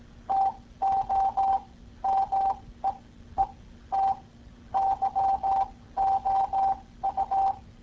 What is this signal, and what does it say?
Morse code, heard as a keyed tone, about 18 wpm, spaced as for 13 wpm, 780 Hz. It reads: TO MEET YOU